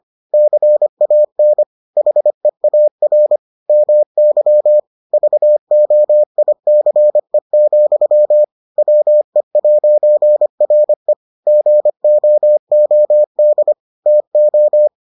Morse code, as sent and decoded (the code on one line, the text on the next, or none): -.-. .- -. / .... .- .-. / -- -.-- / ...- --- .. -.-. --..-- / .-- . .----. .-. / --. --- --- -.. / - ---
CAN HEAR MY VOICE, WE'RE GOOD TO